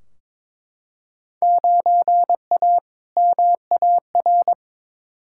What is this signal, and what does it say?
Morse code, heard as a keyed tone, about 21 wpm, 700 Hz. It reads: 9A MAR
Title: Field Day Exchanges Encore - 22wpm